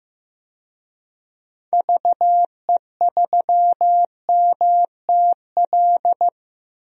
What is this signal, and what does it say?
Morse code, heard as a keyed tone, about 15 wpm, 700 Hz. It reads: VE3MTL
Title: DXCC-100 Call Signs - 6 Characters - Repeated 2x - 15wpm